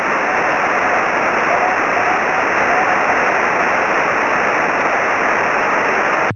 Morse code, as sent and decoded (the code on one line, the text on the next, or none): none